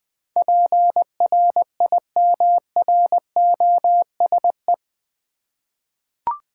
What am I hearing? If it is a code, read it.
Morse code, heard as a keyed tone, about 20 wpm, 700 Hz. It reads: PRIMROSE
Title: Difficult Words with Hints - 20wpm